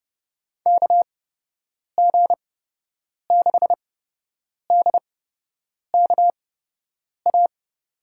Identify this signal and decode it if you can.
Morse code, heard as a keyed tone, about 31 wpm, spaced as for 8 wpm, 700 Hz. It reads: KG6DKA
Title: USA Call Signs - 6 Characters - 8x Character Spacing - 30wpm